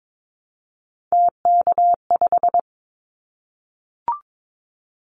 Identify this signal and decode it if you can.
Morse code, heard as a keyed tone, about 22 wpm, 700 Hz. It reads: TK5
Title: Callsign Prefixes for Top 100 DXCC - 22wpm